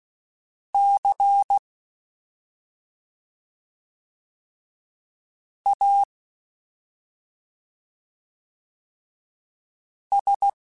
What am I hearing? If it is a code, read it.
Morse code, heard as a keyed tone, about 16 wpm, spaced as for 2 wpm, 800 Hz. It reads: CAS
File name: Special INGs-mp3s / Special INGs-mp3s / Slow ing @10